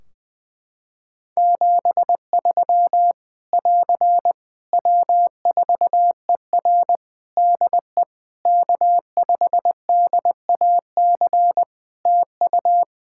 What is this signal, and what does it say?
Morse code, heard as a keyed tone, about 20 wpm, 700 Hz. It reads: 73 <AR> W4ER DE K5DAC TU
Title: K1USN SST QSO Practice 20wpm